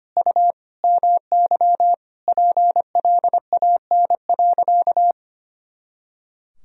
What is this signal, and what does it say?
Morse code, heard as a keyed tone, about 25 wpm, 700 Hz. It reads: U MY PLAN.
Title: Sentences from Top 500 Words and Abbreviations 25wpm